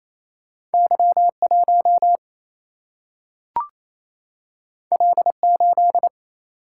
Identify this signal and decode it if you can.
Morse code, heard as a keyed tone, about 28 wpm, 700 Hz. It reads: Y 1 L 8
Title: Single-Letter Single-Number 28wpm